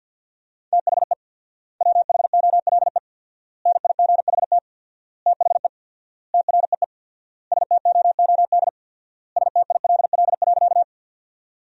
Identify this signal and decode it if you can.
Morse code, heard as a keyed tone, about 50 wpm, 700 Hz. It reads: THE WHOLE NIGHT THE TREE STOOD STILL.
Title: 4x Word-Spacing - Sentences from Top 500 Words - 50wpm